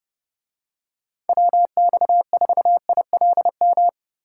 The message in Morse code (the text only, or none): WX4ILM